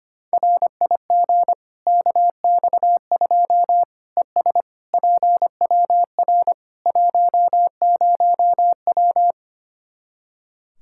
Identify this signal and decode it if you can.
Morse code, heard as a keyed tone, about 25 wpm, 700 Hz. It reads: RIG KX2 ES PWR 10W
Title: K1USN SST QSO Practice 25wpm